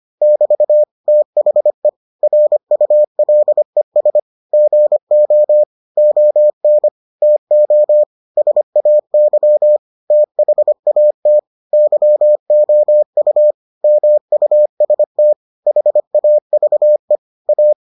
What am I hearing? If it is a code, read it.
Morse code, heard as a keyed tone, about 25 wpm, 600 Hz. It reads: <BT> THE RULES GO ON TO SAY THAT YOU MUST HAVE A